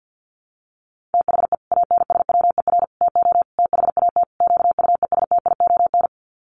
Morse code, heard as a keyed tone, about 48 wpm, 700 Hz. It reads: THE ANSWER TO THAT QUESTION
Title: Sets of 5 Words Encore2 50wpm